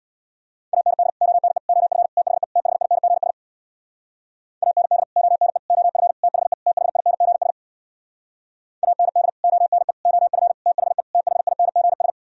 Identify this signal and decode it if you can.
Morse code, heard as a keyed tone, about 64 wpm, 700 Hz. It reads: AND ONE OF THE THINGS AND ONE OF THE THINGS AND ONE OF THE THINGS